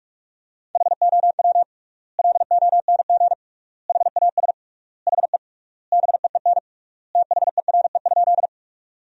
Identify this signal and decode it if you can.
Morse code, heard as a keyed tone, about 45 wpm, 700 Hz. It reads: HOW LONG HAS HE BEEN THERE?
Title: 3x Word-Spacing - Sentences from Top 100 Words - 45wpm